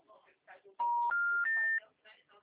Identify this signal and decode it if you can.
Special information tone, with speech.
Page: Carioca 3